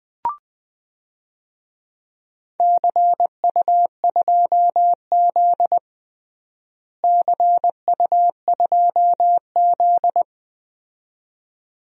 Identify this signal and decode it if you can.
Morse code, heard as a keyed tone, about 20 wpm, 700 Hz. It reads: CU2Z CU2Z
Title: DXCC-100 Call Signs - 4 Characters - Repeated 2x - 20wpm